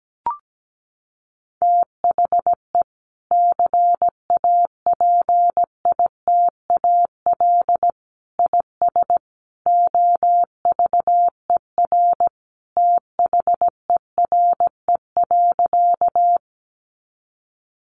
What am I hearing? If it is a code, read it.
Morse code, heard as a keyed tone, about 17 wpm, 700 Hz. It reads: THE CAPITAL IS OVER THERE.
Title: Sentences from Top 1000 Words 17wpm